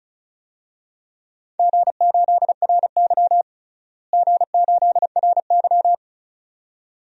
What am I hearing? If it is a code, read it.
Morse code, heard as a keyed tone, about 34 wpm, 700 Hz. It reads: G8RY G8RY
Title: DXCC-100 Call Signs - 4 Characters - Repeated 2x - 35wpm